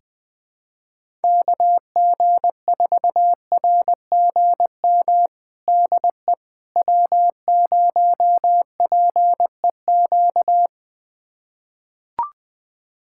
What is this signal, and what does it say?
Morse code, heard as a keyed tone, about 20 wpm, 700 Hz. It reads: KG4RGM DE W0PEQ
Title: USA Callsign DE Callsign 20wpm